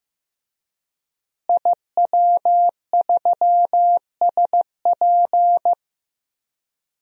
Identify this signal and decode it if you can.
Morse code, heard as a keyed tone, about 15 wpm, 700 Hz. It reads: IW3SP
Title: DXCC-100 Call Signs - 5 Characters - Repeated 2x - 15wpm